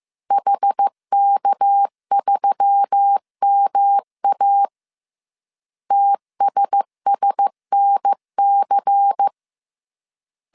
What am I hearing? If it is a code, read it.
Morse code, heard as a keyed tone, about 15 wpm, 800 Hz. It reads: HK3MA TSSNC
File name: CW / DET / 1 6 1500C S C 14wpm 2951